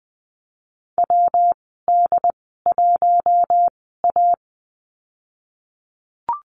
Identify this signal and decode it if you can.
Morse code, heard as a keyed tone, about 20 wpm, spaced as for 14 wpm, 700 Hz. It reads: WD1A
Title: 161 Lesson 26g - Letter g - Call Signs 20wpm